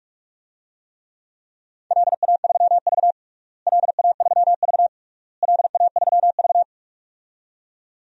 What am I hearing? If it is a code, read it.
Morse code, heard as a keyed tone, about 41 wpm, 700 Hz. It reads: LA3V LA3V LA3V
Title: DXCC-100 Call Signs - 4 Characters - Repeated 3x - 45wpm